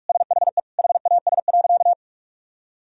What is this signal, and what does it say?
Morse code, heard as a keyed tone, about 46 wpm, 700 Hz. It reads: SHE HAS.
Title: Sentences from Top 1000 Words - Review Part 1 - 45wpm